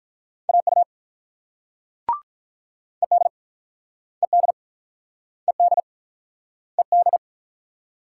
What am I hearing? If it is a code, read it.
Morse code, heard as a keyed tone, about 48 wpm, 690 Hz. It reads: AU ED ED ED ED